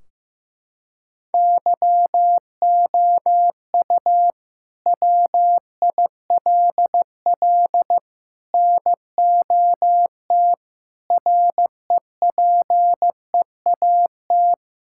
Morse code, heard as a keyed tone, about 15 wpm, 700 Hz. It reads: YOU WILL NOT REPEAT
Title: Sentences from Top 1600 Words 15wpm